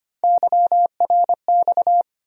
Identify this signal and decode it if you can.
Morse code, heard as a keyed tone, about 25 wpm, 700 Hz. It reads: YRX
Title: USA Call Sign Postfixes - 3 Letters 25wpm